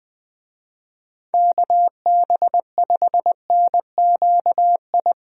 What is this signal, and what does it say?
Morse code, heard as a keyed tone, about 21 wpm, 700 Hz. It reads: KB5NQI
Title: USA Call Signs - 6 Characters - Repeated 2x - 20wpm